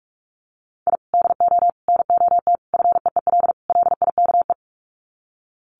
Morse code, heard as a keyed tone, about 44 wpm, 700 Hz. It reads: I DO NOT FEEL LIKE
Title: Sets of 5 Words Encore 45wpm